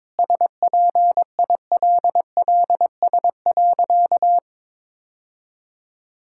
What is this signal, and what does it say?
Morse code, heard as a keyed tone, about 21 wpm, 690 Hz. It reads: SPILLS.
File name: practice / 22wpm / Easy Sentences with Hints - 22wpm